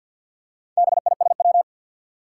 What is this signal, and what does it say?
Morse code, heard as a keyed tone, about 52 wpm, 700 Hz. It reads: BISW
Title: Four Challenging Letter-Number Combinations 50wpm